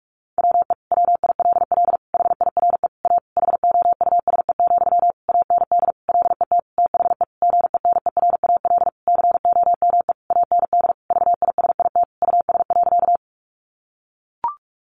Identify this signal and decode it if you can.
Morse code, heard as a keyed tone, about 44 wpm, 700 Hz. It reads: WE WILL HIRE A HOUSE, AND LET THE GENERAL COME AND VISIT US.
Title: Sentences from Top 1400 Words 45wpm